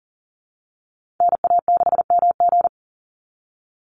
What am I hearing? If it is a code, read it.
Morse code, heard as a keyed tone, about 42 wpm, 700 Hz. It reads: NA6MG
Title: NA SOTA Activator Callsigns 40wpm